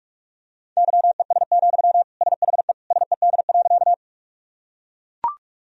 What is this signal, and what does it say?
Morse code, heard as a keyed tone, about 42 wpm, 700 Hz. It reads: YES, SHE SED.
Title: Sentences from Top 500 Words and Abbreviations 45wpm